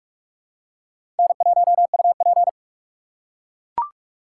Morse code, heard as a keyed tone, about 41 wpm, 700 Hz. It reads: N1UP
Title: USA Call Signs - 4 Characters - 4x Character Spacing - 45wpm